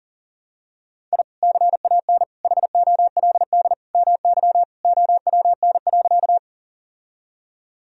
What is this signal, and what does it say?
Morse code, heard as a keyed tone, about 37 wpm, 700 Hz. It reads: I CAN HOLD MY OWN.
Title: Sentences from Top 1000 Words - Review Part 2 - 40wpm